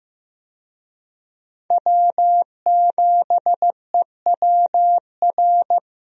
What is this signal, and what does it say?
Morse code, heard as a keyed tone, about 15 wpm, 700 Hz. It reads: W7EWR
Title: USA Call Signs - 5 Characters - Repeated 2x - 15wpm